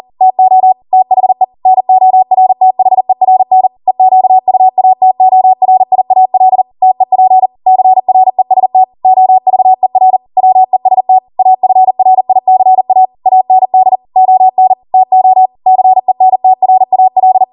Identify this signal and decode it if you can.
Morse code, heard as a keyed tone, about 40 wpm, 750 Hz. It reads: TO THE NORTHERN EQUATORIAL TEP CREST OVER WEST AFRICA AND ON TO CENTRAL